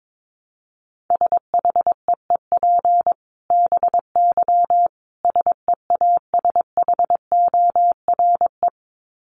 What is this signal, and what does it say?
Morse code, heard as a keyed tone, about 22 wpm, 700 Hz. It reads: SHEEP BY SEASHORE
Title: Dit Challenge 22wpm